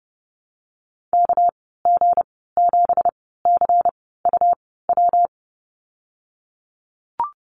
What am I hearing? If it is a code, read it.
Morse code, heard as a keyed tone, about 30 wpm, spaced as for 17 wpm, 700 Hz. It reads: KG7CUW
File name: Lesson 37g - Letter x - Call Signs 30wpm